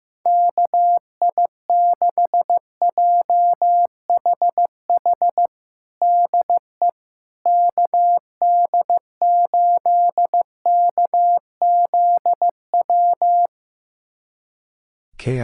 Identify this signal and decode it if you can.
Morse code, heard as a keyed tone, about 15 wpm, 700 Hz. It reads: KI6JHH DE KD8KZW
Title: USA Callsign DE Callsign 15wpm